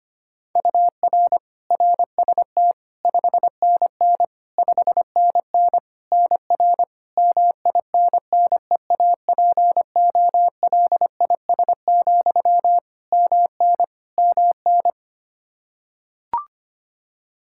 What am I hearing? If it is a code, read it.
Morse code, heard as a keyed tone, about 25 wpm, 700 Hz. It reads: UR RST 5NN 5NN NR MINNEAPOLIS, MN MN
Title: Practice QSO - Exchange 1 - Beginning - 25wpm